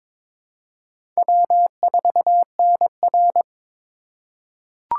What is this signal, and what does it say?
Morse code, heard as a keyed tone, about 22 wpm, 700 Hz. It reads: W4NR
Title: USA Call Signs - 2x Character Spacing - 22wpm